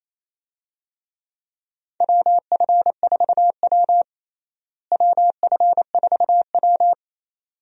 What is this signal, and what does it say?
Morse code, heard as a keyed tone, about 28 wpm, 700 Hz. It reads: WF4W WF4W
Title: USA Call Signs - 4 Characters - Repeated 2x - 28wpm